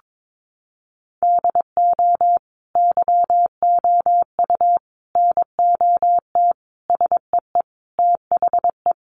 Morse code, heard as a keyed tone, about 22 wpm, 700 Hz. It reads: DO YOU NOT SEE THE
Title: Sentences from Top 800 Words 22wpm